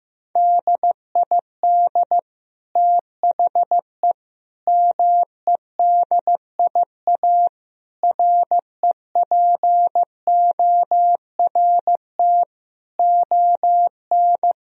Morse code, heard as a keyed tone, about 15 wpm, 700 Hz. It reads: DID THE MEDIA REPORT ON
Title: Sentences from Top 1000 Words - Review Part 4 - 15wpm